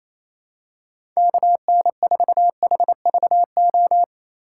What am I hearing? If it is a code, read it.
Morse code, heard as a keyed tone, about 28 wpm, 700 Hz. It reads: KN4HVO
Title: USA Call Signs - 6 Characters - 4x Character Spacing - 28wpm